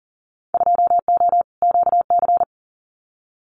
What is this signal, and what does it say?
Morse code, heard as a keyed tone, about 40 wpm, 700 Hz. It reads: 2O QC
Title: Winter Field Day Exchanges 40wpm